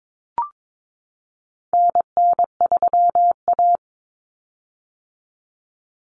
Morse code, heard as a keyed tone, about 22 wpm, 700 Hz. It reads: NN3A
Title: Common Cut Numbers - 4-Digit 22wpm